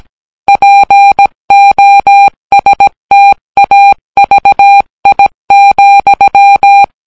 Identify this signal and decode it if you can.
Morse code, heard as a keyed tone, about 17 wpm, 800 Hz. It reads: POSTAVI,